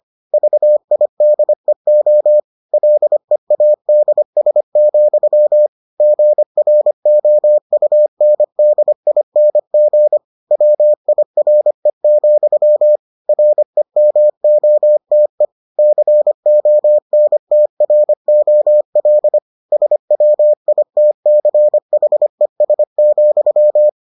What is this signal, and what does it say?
Morse code, heard as a keyed tone, about 25 wpm, 600 Hz. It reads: VIDEO LEADS, GROUNDING WIRE, REMOTE CONTROL SWITCHES,